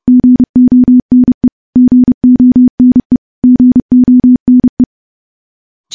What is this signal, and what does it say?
Morse code, heard as a keyed tone, about 32 wpm, 250 Hz. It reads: GONE GONE GONE
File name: morse / morse-training-30WPM